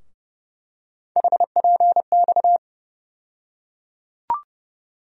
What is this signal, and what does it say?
Morse code, heard as a keyed tone, about 32 wpm, 700 Hz. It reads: HPX
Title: USA Call Sign Postfixes - 3 Letters 30wpm